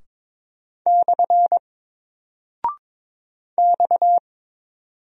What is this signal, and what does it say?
Morse code, heard as a keyed tone, about 22 wpm, 700 Hz. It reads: / X
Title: Single Character (Letter:Number:Punctuation:Prosign) 22wpm